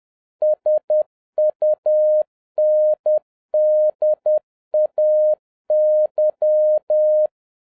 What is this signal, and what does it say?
Morse code, heard as a keyed tone, about 10 wpm, 610 Hz. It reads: SUNDAY
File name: 2011-07-mp3 / n36 THE 3.11 2 10wpm